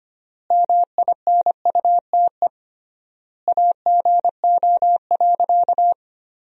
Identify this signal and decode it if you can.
Morse code, heard as a keyed tone, about 25 wpm, 700 Hz. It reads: MINUTE AGO.
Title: 3x Word-Spacing - Sentences from Top 500 Words - 25wpm